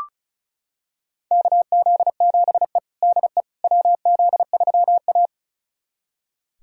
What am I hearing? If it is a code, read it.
Morse code, heard as a keyed tone, about 34 wpm, 700 Hz. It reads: KZ7E DE WZ3A
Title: USA Callsign DE Callsign - 4 Characters - 35wpm